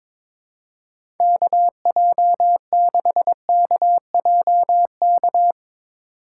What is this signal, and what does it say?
Morse code, heard as a keyed tone, about 22 wpm, 700 Hz. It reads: KJ6KJK
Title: USA Call Signs - 6 Characters - 8x Character Spacing - 22wpm